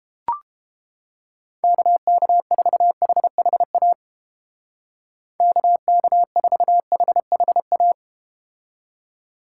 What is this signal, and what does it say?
Morse code, heard as a keyed tone, about 30 wpm, 700 Hz. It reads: KK4HHA KK4HHA